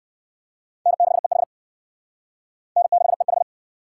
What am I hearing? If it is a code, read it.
Morse code, heard as a keyed tone, about 68 wpm, 700 Hz. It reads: N6EH N6EH